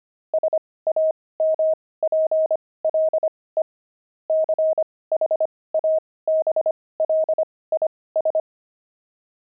Morse code, heard as a keyed tone, about 25 wpm, spaced as for 18 wpm, 630 Hz. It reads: SAMPLE CHABLIS
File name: pangram-fs-18-cs-25-wine